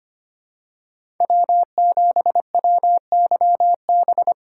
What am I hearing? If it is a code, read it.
Morse code, heard as a keyed tone, about 25 wpm, 700 Hz. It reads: W7WYB
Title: USA Call Signs - 5 Characters - 4x Character Spacing - 25wpm